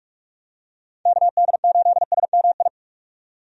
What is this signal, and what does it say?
Morse code, heard as a keyed tone, about 43 wpm, 700 Hz. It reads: KD8SMI
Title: USA Call Signs - 6 Characters - 45wpm